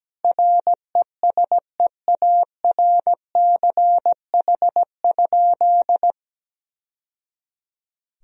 Morse code, heard as a keyed tone, about 17 wpm, 700 Hz. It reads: RESEARCH?
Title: Sentences from Top 1000 Words - Review Part 4 - 17wpm